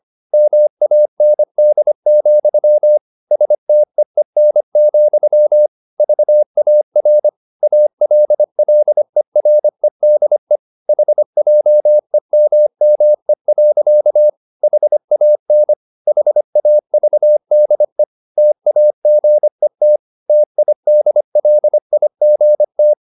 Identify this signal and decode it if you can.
Morse code, heard as a keyed tone, about 25 wpm, 600 Hz. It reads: MAND, STEEN, VAR ALLEREDE HJEMME. HAN HAVDE TAGET TIDLIGT
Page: Ferie 25 wpm